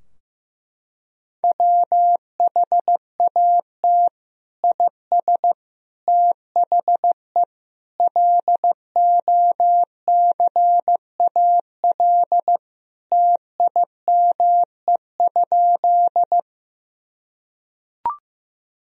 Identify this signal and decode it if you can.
Morse code, heard as a keyed tone, about 14 wpm, 700 Hz. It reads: WHAT IS THE LOCAL TIME?
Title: Sentences from Top 1000 Words - Review Part 3 - 15wpm